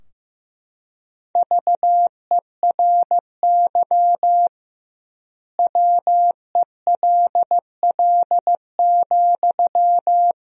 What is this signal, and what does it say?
Morse code, heard as a keyed tone, about 15 wpm, 700 Hz. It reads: VERY WELL,
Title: 2x Word-Spacing - Sentences from Top 500 Words - 15wpm